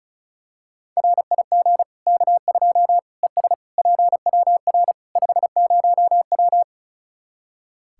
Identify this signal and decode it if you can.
Morse code, heard as a keyed tone, about 36 wpm, 700 Hz. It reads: RIG K2 ES PWR 50W
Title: Practice QSO - Exchange 2 - Info - 35wpm